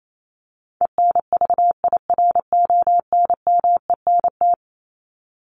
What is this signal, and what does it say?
Morse code, heard as a keyed tone, about 28 wpm, 700 Hz. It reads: ENVIRONMENT
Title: Top 501-750 Words 28wpm